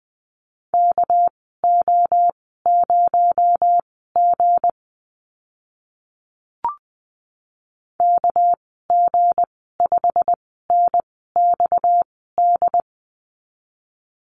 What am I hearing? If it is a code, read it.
Morse code, heard as a keyed tone, about 20 wpm, spaced as for 14 wpm, 700 Hz. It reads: KO0G KG5NXD
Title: 231 Lesson 37g - Letter x - Call Signs 20wpm